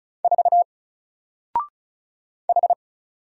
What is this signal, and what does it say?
Morse code, heard as a keyed tone, about 36 wpm, 700 Hz. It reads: 4 H
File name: Single-Letter-Number-35wpm